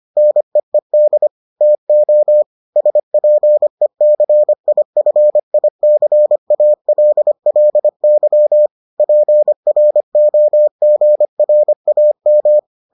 Morse code, heard as a keyed tone, about 25 wpm, 600 Hz. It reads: NEED TO SPECIFICALLY PROGRAM